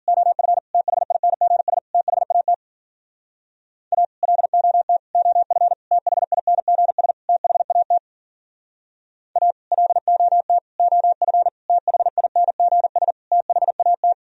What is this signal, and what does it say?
Morse code, heard as a keyed tone, about 47 wpm, 700 Hz. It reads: OF THINGS THAT A LOT OF THINGS THAT A LOT OF THINGS THAT
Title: Speed-Racing - Sets of 5 Words 40wpm